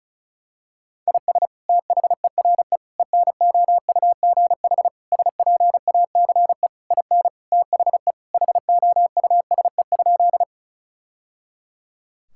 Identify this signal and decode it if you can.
Morse code, heard as a keyed tone, about 36 wpm, 700 Hz. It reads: IS THERE ENOUGH SPACE IN THE HOUSE?
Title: Sentences from Top 1000 Words - Review Part 5 - 35wpm